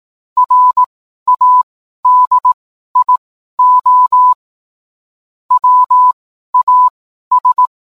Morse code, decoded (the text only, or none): RADIO WAS